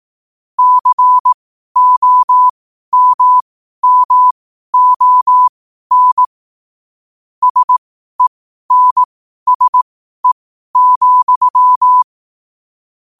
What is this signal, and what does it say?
Morse code, heard as a keyed tone, about 18 wpm, spaced as for 12 wpm, 1000 Hz. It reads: COMMON SENSE,
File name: qotdfiles / 12wpm